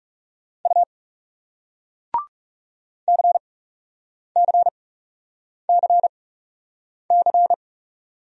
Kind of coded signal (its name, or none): Morse code